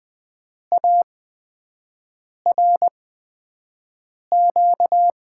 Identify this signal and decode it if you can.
Morse code, heard as a keyed tone, about 20 wpm, spaced as for 5 wpm, 700 Hz. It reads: ARQ
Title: USA Call Signs - 6 Characters - 8x Character Spacing - 20wpm